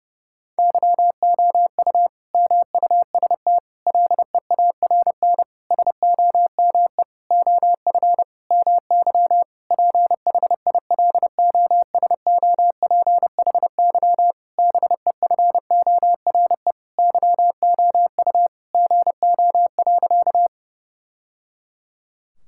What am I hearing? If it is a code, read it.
Morse code, heard as a keyed tone, about 30 wpm, 700 Hz. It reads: YOU MUST LEARN SOME OF MY PHILOSOPHY BEFORE YOU GO.
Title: Sentences from Top 1000 Words - Review Part 6 - 30wpm